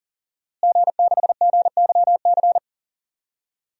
Morse code, heard as a keyed tone, about 40 wpm, 700 Hz. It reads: G6GYC
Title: DXCC-100 Call Signs - 5 Characters - Repeated 2x - 40wpm